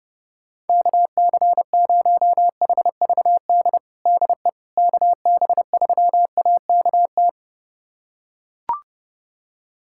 Morse code, heard as a keyed tone, about 30 wpm, 700 Hz. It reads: KC0HVD DE KB3AKT
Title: USA Callsign DE Callsign 30wpm